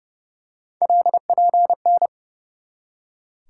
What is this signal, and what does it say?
Morse code, heard as a keyed tone, about 30 wpm, 700 Hz. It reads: LPN
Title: Practice QSO - Exchange 3 - More Info - 30wpm